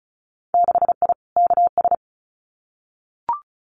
Morse code, heard as a keyed tone, about 34 wpm, 700 Hz. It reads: 6I KS
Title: Winter Field Day Exchanges 35wpm